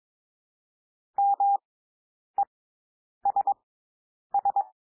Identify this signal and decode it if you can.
Morse code, heard as a keyed tone, about 22 wpm, spaced as for 9 wpm, 800 Hz. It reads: MESS
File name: SOUND WORDS 22 wpm